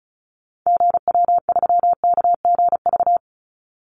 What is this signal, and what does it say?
Morse code, heard as a keyed tone, about 34 wpm, 700 Hz. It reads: GW3KGV
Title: Call Signs Repeated 3x - 35wpm